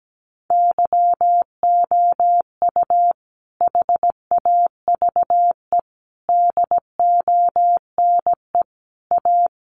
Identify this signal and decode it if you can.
Morse code, heard as a keyed tone, about 17 wpm, 700 Hz. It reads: YOU HAVE DONE A
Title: Sentences from Top 700 Words 17wpm